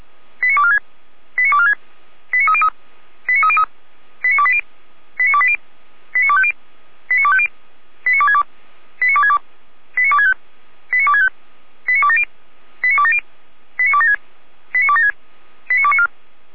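Five-tone selective call (ZVEI), heard as quick runs of five tones, tones of about 80 ms. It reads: 89246 89246 89303 89303 89280 89280 89250 89250 89262 89262 89265 89265 89270 89270 89276 89276 89294
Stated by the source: ZVEI five-tone selective call 89208, received by SDR.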